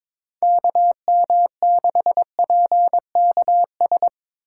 Morse code, heard as a keyed tone, about 24 wpm, 700 Hz. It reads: KM6PKS
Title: Speed Racing - USA Call Signs 22wpm